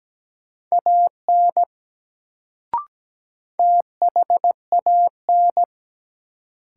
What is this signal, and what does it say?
Morse code, heard as a keyed tone, about 17 wpm, 700 Hz. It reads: AN THAN
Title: Sets of 2 Words - New Word Drill - 17wpm